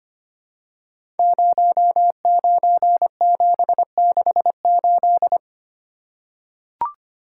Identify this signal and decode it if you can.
Morse code, heard as a keyed tone, about 25 wpm, 700 Hz. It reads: 09768
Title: Code Groups - 5 Letter:Numbers - 25wpm